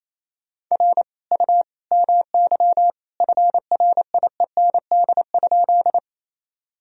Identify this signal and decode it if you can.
Morse code, heard as a keyed tone, about 28 wpm, 700 Hz. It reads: R U MY FRIEND?
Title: Sentences from Top 300 Words and Abbreviations 28wpm